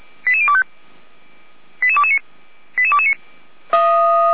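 Five-tone selective call (ZVEI), heard as quick runs of five tones, tones of about 70 ms. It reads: 88025 88229 88229
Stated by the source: ZVEI five-tone selective call received on an SDR, from the Feuerwehr Orlamünde fire brigade audio server.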